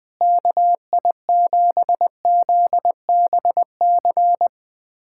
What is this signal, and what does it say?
Morse code, heard as a keyed tone, about 20 wpm, 700 Hz. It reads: KI7ZBC
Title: USA Callsign DE Callsign 20wpm